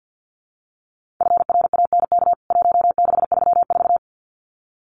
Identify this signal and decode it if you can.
Morse code, heard as a keyed tone, about 49 wpm, 700 Hz. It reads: FRANK 1634
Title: CWT Exchange 50wpm